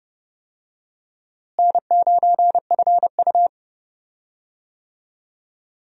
Morse code, heard as a keyed tone, about 30 wpm, 700 Hz. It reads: N9FU
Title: USA Call Signs - 4 Characters - Repeated 3x - 30wpm